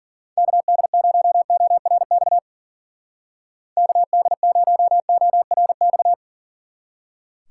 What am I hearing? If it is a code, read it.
Morse code, heard as a keyed tone, about 49 wpm, 700 Hz. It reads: KD0ORX KD0ORX
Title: Speed Racing - USA Call Signs 40wpm